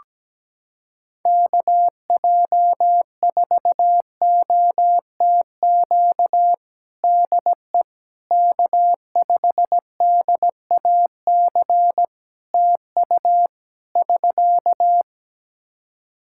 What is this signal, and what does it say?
Morse code, heard as a keyed tone, about 17 wpm, 700 Hz. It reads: KJ4OTQ DE K5DAC TU <SK>
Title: Practice QSO - Exchange 4 - Ending - 17wpm